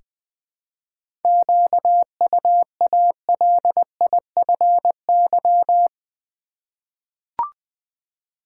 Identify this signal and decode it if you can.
Morse code, heard as a keyed tone, about 20 wpm, 700 Hz. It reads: QUALIFY
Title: Seven Letter Words - 20wpm